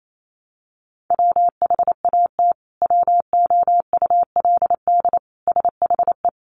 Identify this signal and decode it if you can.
Morse code, heard as a keyed tone, about 28 wpm, 700 Hz. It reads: WHAT WOULD SHE
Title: Sentences from Top 100 Words 28wpm